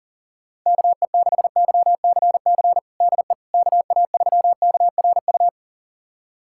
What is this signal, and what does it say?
Morse code, heard as a keyed tone, about 40 wpm, 700 Hz. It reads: KE6YCC DE KA3KRU
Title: USA Callsign DE Callsign 40wpm